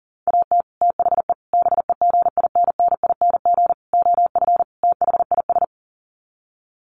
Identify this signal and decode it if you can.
Morse code, heard as a keyed tone, about 40 wpm, 700 Hz. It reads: AT THE BEGINNING OF THIS
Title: Sets of 5 Words Encore 40wpm